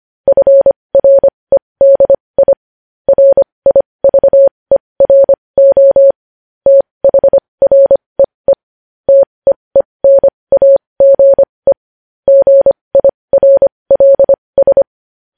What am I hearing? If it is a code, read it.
Morse code, heard as a keyed tone, about 25 wpm, spaced as for 20 wpm, 550 Hz. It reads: FREDI RIVERO THREE TEENAGE GIRLS
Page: news-headlines-in-morse-code-at-20-wpm